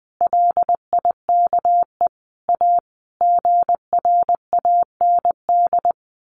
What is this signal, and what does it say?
Morse code, heard as a keyed tone, about 20 wpm, 700 Hz. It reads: LIKE A GRAND
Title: A Sound Of Thunder 20wpm